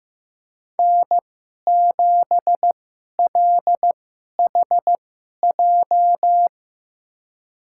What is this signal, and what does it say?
Morse code, heard as a keyed tone, about 15 wpm, spaced as for 11 wpm, 700 Hz. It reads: N7LHJ
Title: USA Call Signs - 5 Characters - 2x Character Spacing - 15wpm